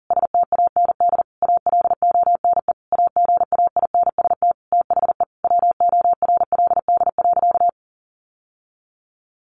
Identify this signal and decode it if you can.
Morse code, heard as a keyed tone, about 40 wpm, 700 Hz. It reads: STAND ALONE AGAINST THE WORLD.
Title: Sentences from Top 1400 Words 40wpm